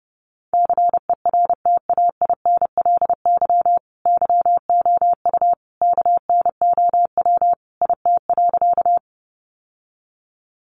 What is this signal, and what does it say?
Morse code, heard as a keyed tone, about 30 wpm, 700 Hz. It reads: CERTAINLY YOU KNOW IT.
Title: Sentences from Top 900 Words 30wpm